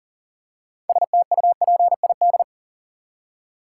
Morse code, heard as a keyed tone, about 40 wpm, 700 Hz. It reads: STUPID
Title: Six Letter Words - 40wpm